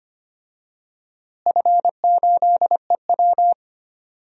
Morse code, heard as a keyed tone, about 25 wpm, 700 Hz. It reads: F8EW
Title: DXCC-100 Call Signs - 4 Characters - Repeated 3x - 25wpm